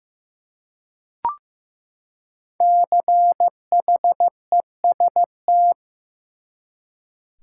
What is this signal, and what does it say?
Morse code, heard as a keyed tone, about 15 wpm, 700 Hz. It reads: CHEST
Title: Five Letter Words - 15wpm